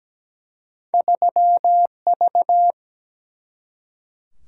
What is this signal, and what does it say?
Morse code, heard as a keyed tone, about 17 wpm, 700 Hz. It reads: 3 V
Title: Single-Letter Single-Number 17wpm